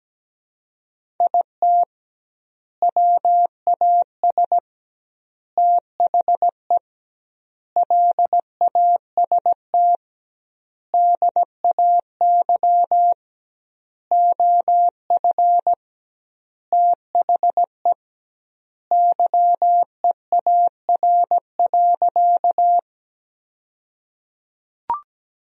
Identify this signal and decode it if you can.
Morse code, heard as a keyed tone, about 17 wpm, 700 Hz. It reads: IT WAS THE LAST DAY OF THE YEAR.
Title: 2x Word-Spacing - Sentences from Top 300 Words - 17wpm